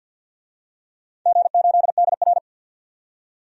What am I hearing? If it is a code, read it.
Morse code, heard as a keyed tone, about 49 wpm, 700 Hz. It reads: G8DR
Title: DXCC-100 Call Signs - 4 Characters - 50wpm